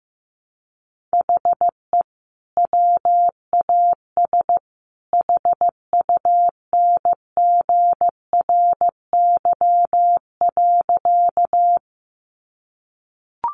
Morse code, heard as a keyed tone, about 15 wpm, 700 Hz. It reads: HE WAS HUNGRY.
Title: Sentences from Top 1400 Words 15wpm